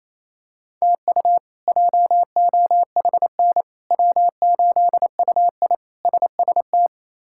Morse code, heard as a keyed tone, about 27 wpm, 700 Hz. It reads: TU JOHN W8UI SST
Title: K1USN SST QSO Practice - 28wpm